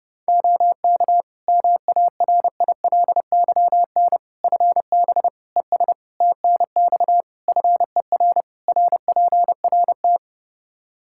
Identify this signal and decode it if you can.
Morse code, heard as a keyed tone, about 30 wpm, 700 Hz. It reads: OK MARILYN FB ES TNX FER RPRT
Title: Practice QSO - Exchange 2 - Info - 30wpm